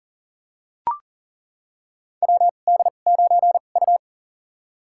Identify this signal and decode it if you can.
Morse code, heard as a keyed tone, about 40 wpm, spaced as for 29 wpm, 700 Hz. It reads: WD9U
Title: USA Call Signs - 2x Character Spacing - 40wpm